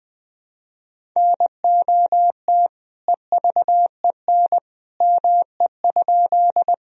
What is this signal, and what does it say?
Morse code, heard as a keyed tone, about 20 wpm, 700 Hz. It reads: NOT EVEN ME?
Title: Sentences from Top 1000 Words - Review Part 1 - 20wpm